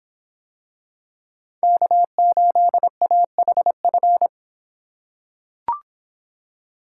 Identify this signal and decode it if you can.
Morse code, heard as a keyed tone, about 26 wpm, 700 Hz. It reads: K8AHF